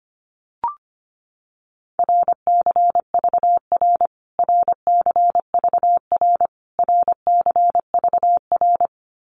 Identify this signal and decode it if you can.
Morse code, heard as a keyed tone, about 25 wpm, 700 Hz. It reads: RCVR RCVR RCVR
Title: Abbreviations Repeated 3x - 25wpm